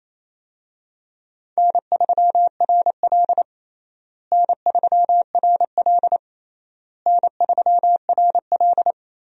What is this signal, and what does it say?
Morse code, heard as a keyed tone, about 28 wpm, 700 Hz. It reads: N3RL N3RL N3RL
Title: USA Call Signs - 4 Characters - Repeated 3x - 28wpm